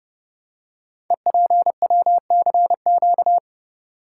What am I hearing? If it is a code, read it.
Morse code, heard as a keyed tone, about 30 wpm, 700 Hz. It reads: EPWCQ
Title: Code Groups - 5 Letter:Numbers - 30wpm